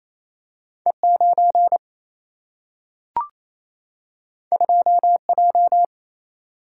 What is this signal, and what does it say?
Morse code, heard as a keyed tone, about 28 wpm, 700 Hz. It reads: E9 2J
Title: Single-Letter Single-Number 28wpm